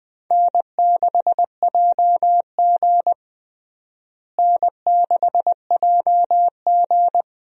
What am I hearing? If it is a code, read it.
Morse code, heard as a keyed tone, about 20 wpm, 700 Hz. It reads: N6JG N6JG
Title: USA Call Signs - 4 Characters - Repeated 2x - 20wpm